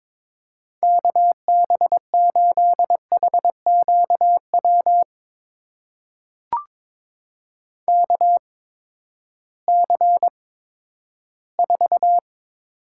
Morse code, heard as a keyed tone, about 22 wpm, spaced as for 13 wpm, 700 Hz. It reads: KB8HQW KC4
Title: USA Call Signs - 6 Characters - 8x Character Spacing - 22wpm